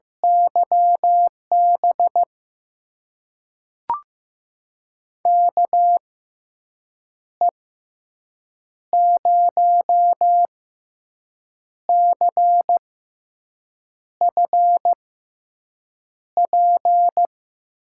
Morse code, heard as a keyed tone, about 15 wpm, spaced as for 6 wpm, 700 Hz. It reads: YB KE0CFP
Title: USA Call Signs - 6x Character Spacing - 15wpm